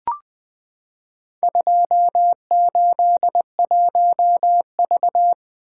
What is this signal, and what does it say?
Morse code, heard as a keyed tone, about 20 wpm, 700 Hz. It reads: 281V
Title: Four Letter-Number Combinations - 20wpm